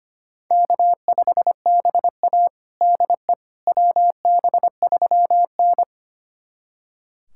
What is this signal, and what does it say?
Morse code, heard as a keyed tone, about 25 wpm, 700 Hz. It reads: K5BA DE WB3N
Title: USA Callsign DE Callsign - 4 Characters - 25wpm